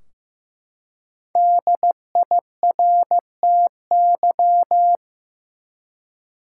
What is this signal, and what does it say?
Morse code, heard as a keyed tone, about 15 wpm, 700 Hz. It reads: DIRTY
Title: Five Letter Words - 15wpm